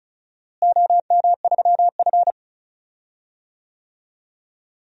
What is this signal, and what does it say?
Morse code, heard as a keyed tone, about 36 wpm, 700 Hz. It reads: OM3F
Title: DXCC-100 Call Signs - 4 Characters - Repeated 2x - 35wpm